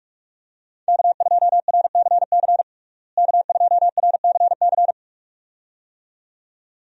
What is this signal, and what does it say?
Morse code, heard as a keyed tone, about 41 wpm, 700 Hz. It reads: K2RCC K2RCC
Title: USA Call Signs - 5 Characters - Repeated 2x - 45wpm